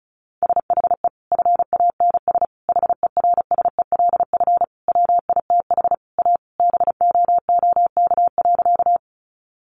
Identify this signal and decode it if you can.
Morse code, heard as a keyed tone, about 34 wpm, 700 Hz. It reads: SHE FANS HERSELF WITH A BOOK.
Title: Additional Easy Phrases 35wpm